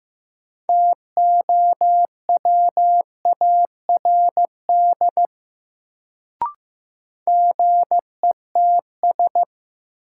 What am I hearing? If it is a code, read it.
Morse code, heard as a keyed tone, about 15 wpm, 700 Hz. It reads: TOWARD GETS
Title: Sentences from Top 1300 Words - New Word Drill Alt - 15wpm